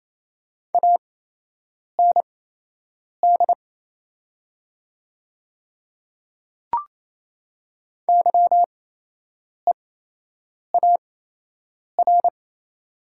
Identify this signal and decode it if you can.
Morse code, heard as a keyed tone, about 27 wpm, spaced as for 8 wpm, 690 Hz. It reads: AND YEAR
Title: 8x Character Spacing - Top 300 Words - 28wpm